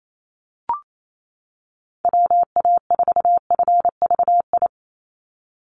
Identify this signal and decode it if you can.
Morse code, heard as a keyed tone, about 28 wpm, 700 Hz. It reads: WA4FVI
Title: USA Call Signs 28wpm